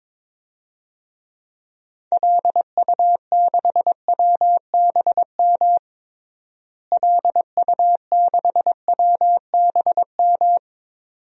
Morse code, heard as a keyed tone, about 22 wpm, 700 Hz. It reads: LU6WBM LU6WBM
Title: DXCC-100 Call Signs - 6 Characters - Repeated 2x - 22wpm